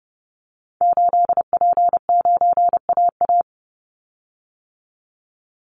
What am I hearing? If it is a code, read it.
Morse code, heard as a keyed tone, about 30 wpm, 700 Hz. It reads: 8P9AA
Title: Call Signs - 30wpm